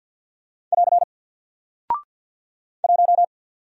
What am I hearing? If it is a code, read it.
Morse code, heard as a keyed tone, about 49 wpm, 700 Hz. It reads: <AR> 1
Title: Single Character (Letter:Number:Punctuation:Prosign) 50wpm